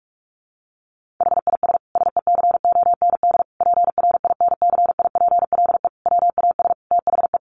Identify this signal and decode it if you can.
Morse code, heard as a keyed tone, about 44 wpm, 700 Hz. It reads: HIS SECOND PRINCIPLE WAS THE
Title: Sentences from Top 1400 Words 45wpm